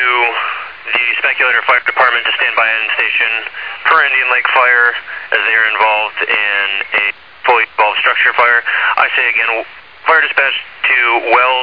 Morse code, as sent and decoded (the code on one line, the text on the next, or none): none